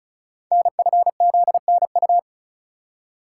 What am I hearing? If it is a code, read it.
Morse code, heard as a keyed tone, about 36 wpm, 690 Hz. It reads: NFZNU
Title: Code Groups - 5 Letters - 35wpm